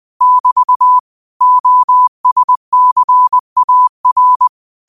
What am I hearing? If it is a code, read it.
Morse code, heard as a keyed tone, about 20 wpm, 1000 Hz. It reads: <BT> OSCAR
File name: qotdfiles / 20wpm